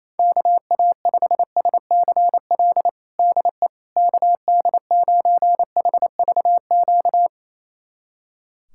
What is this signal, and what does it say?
Morse code, heard as a keyed tone, about 28 wpm, 700 Hz. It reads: KA5SCL DE KD9HVQ
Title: USA Callsign DE Callsign - 6 Characters - 28wpm